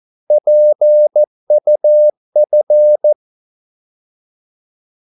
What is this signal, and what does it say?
Morse code, heard as a keyed tone, about 14 wpm, 600 Hz. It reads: PUF
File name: Korte ord DK 14wpm